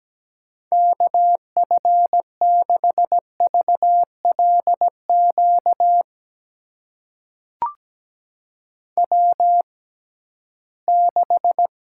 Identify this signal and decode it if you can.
Morse code, heard as a keyed tone, about 17 wpm, spaced as for 13 wpm, 700 Hz. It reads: KF6VLQ W6
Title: USA Call Signs - 6x Character Spacing - 17wpm